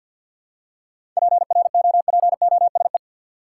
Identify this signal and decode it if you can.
Morse code, heard as a keyed tone, about 47 wpm, 700 Hz. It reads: PROPOSE
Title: Seven Letter Words - 50wpm